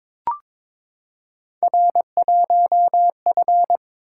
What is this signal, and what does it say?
Morse code, heard as a keyed tone, about 22 wpm, 700 Hz. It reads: R1F
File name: practice / 22wpm / Three Letter-Number Combinations - 22wpm